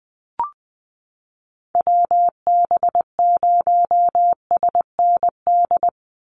Morse code, heard as a keyed tone, about 20 wpm, 700 Hz. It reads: WB0SND
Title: Call Signs Repeated 3x - 20wpm